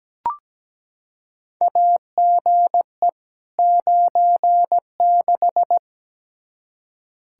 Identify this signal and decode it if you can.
Morse code, heard as a keyed tone, about 17 wpm, 700 Hz. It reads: AGE 96
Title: Practice QSO - Exchange 3 - More Info - 17wpm